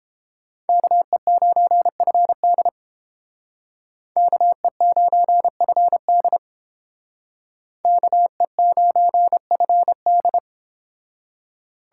Morse code, heard as a keyed tone, about 30 wpm, 700 Hz. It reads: KE9FD KE9FD KE9FD